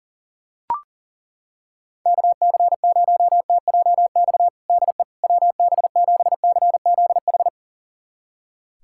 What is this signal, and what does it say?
Morse code, heard as a keyed tone, about 40 wpm, 700 Hz. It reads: KC0TJX DE WB7CZH
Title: USA Callsign DE Callsign 40wpm